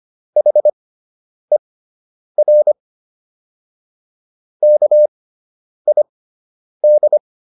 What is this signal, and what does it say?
Morse code, heard as a keyed tone, about 25 wpm, spaced as for 9 wpm, 600 Hz. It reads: HER KID